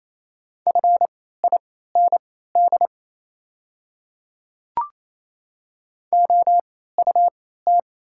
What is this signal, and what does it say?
Morse code, heard as a keyed tone, about 28 wpm, spaced as for 16 wpm, 690 Hz. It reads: FIND OUT
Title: 3x Character Spacing - Top 100 Words - 28wpm